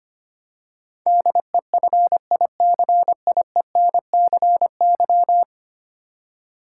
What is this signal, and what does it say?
Morse code, heard as a keyed tone, about 25 wpm, 700 Hz. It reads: DEFICIENCY
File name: Ten Letter Words - 25wpm